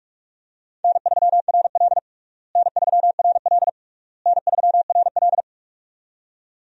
Morse code, heard as a keyed tone, about 46 wpm, 700 Hz. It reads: N3RL N3RL N3RL
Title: USA Call Signs - 4 Characters - Repeated 3x - 45wpm